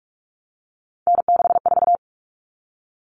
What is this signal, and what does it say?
Morse code, heard as a keyed tone, about 44 wpm, 700 Hz. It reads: N64
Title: Common Cut Numbers - 3-Digit 45wpm